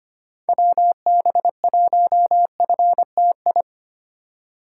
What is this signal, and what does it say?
Morse code, heard as a keyed tone, about 25 wpm, 700 Hz. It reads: WB1FTI
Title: USA Call Signs - 6 Characters - 2x Character Spacing - 25wpm